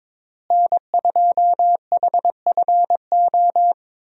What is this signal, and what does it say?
Morse code, heard as a keyed tone, about 22 wpm, 700 Hz. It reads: N2HFO
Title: USA Call Signs - 5 Characters - Repeated 2x - 22wpm